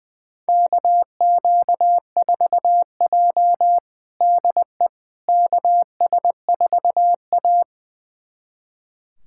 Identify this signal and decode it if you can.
Morse code, heard as a keyed tone, about 21 wpm, 700 Hz. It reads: KQ4J DE KS4A
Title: USA Callsign DE Callsign - 4 Characters - 20wpm